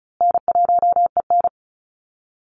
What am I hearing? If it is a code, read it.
Morse code, heard as a keyed tone, about 34 wpm, 700 Hz. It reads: N1EN
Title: Call Signs Repeated 2x - 35wpm